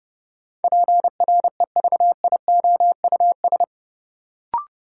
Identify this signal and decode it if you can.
Morse code, heard as a keyed tone, about 30 wpm, 700 Hz. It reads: PREVIOUS
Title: Sentences from Top 1500 Words - New Word Drill Alt - 30wpm